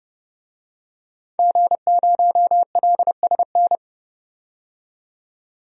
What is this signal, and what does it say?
Morse code, heard as a keyed tone, about 31 wpm, 700 Hz. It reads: G0LSN
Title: DXCC-100 Call Signs - 5 Characters - 30wpm